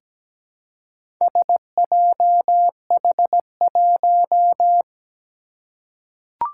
Morse code, heard as a keyed tone, about 17 wpm, 700 Hz. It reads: SJH1
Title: Four Challenging Letter-Number Combinations 17wpm